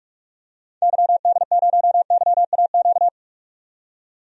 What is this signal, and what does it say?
Morse code, heard as a keyed tone, about 46 wpm, 700 Hz. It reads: YD0YAQ
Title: DXCC-100 Call Signs - 6 Characters - 45wpm